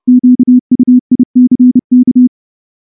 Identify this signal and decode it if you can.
Morse code, heard as a keyed tone, about 30 wpm, 250 Hz. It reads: QUICK
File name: morse-training-30WPM